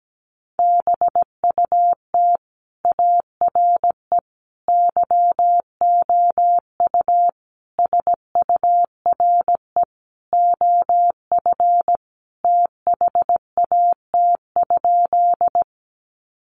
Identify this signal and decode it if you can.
Morse code, heard as a keyed tone, about 17 wpm, 700 Hz. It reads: BUT ARE YOU SURE OF THAT?
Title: Sentences from Top 300 Words 17wpm